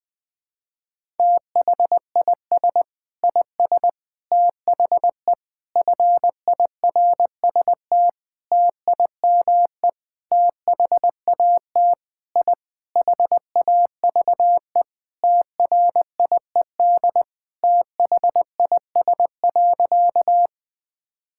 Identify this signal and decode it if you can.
Morse code, heard as a keyed tone, about 20 wpm, 700 Hz. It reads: THIS IS THE FIRST TIME THAT I HAVE TRIED THIS.
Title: Sentences with 6-Word Phrases - 20wpm